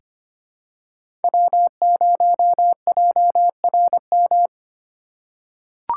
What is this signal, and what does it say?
Morse code, heard as a keyed tone, about 25 wpm, 700 Hz. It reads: W0JRM
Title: USA Call Signs - 5 Characters - 2x Character Spacing - 25wpm